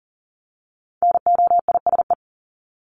Morse code, heard as a keyed tone, about 40 wpm, 700 Hz. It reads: NOISE
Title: Top 1501-1750 Words 40wpm